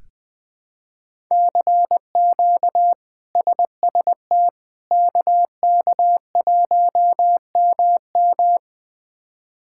Morse code, heard as a keyed tone, about 20 wpm, 700 Hz. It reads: CQ SST KK1MM
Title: K1USN SST QSO Practice - 20wpm